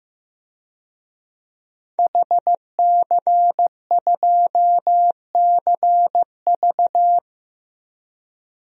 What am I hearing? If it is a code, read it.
Morse code, heard as a keyed tone, about 15 wpm, 700 Hz. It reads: HC2CV
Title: DXCC-100 Call Signs - 5 Characters - 15wpm